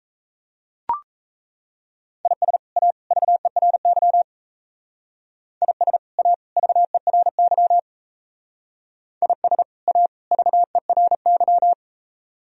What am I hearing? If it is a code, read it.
Morse code, heard as a keyed tone, about 37 wpm, 700 Hz. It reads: IS A VERY IS A VERY IS A VERY